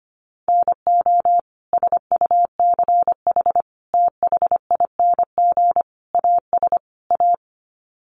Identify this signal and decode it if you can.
Morse code, heard as a keyed tone, about 25 wpm, 700 Hz. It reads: NO SUCH THING AS A
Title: Sets of 5 Words Encore 25wpm